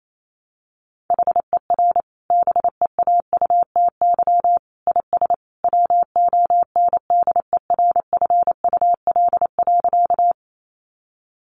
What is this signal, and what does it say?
Morse code, heard as a keyed tone, about 28 wpm, 700 Hz. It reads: HER BEAUTY IS WONDERFUL.
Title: Sentences from Top 1100 Words 28wpm